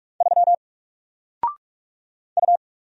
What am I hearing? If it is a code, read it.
Morse code, heard as a keyed tone, about 44 wpm, 700 Hz. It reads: 3 U
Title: Single Character (Letter:Number:Punctuation:Prosign) Alt 45wpm